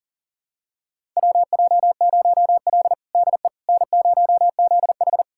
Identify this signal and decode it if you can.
Morse code, heard as a keyed tone, about 40 wpm, 700 Hz. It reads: WJ0L DE N0ZH
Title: USA Callsign DE Callsign - 4 Characters - 40wpm